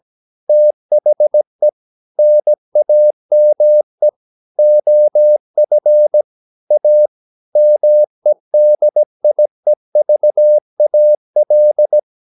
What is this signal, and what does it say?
Morse code, heard as a keyed tone, about 17 wpm, 600 Hz. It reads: THE NAME OF A MEDIEVAL
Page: Romanen Never af Ken Follet på engelsk.